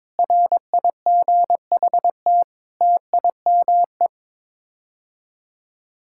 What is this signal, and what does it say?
Morse code, heard as a keyed tone, about 22 wpm, 700 Hz. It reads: RIGHT TIME
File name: Sets of 9 Words - 22wpm